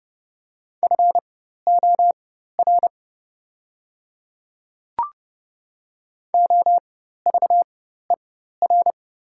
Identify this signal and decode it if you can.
Morse code, heard as a keyed tone, about 30 wpm, spaced as for 14 wpm, 700 Hz. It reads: FOR OVER
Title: 4x Character Spacing - Top 100 Words - 30wpm